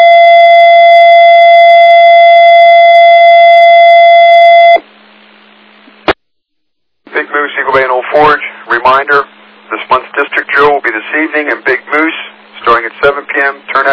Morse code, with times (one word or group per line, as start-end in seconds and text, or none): none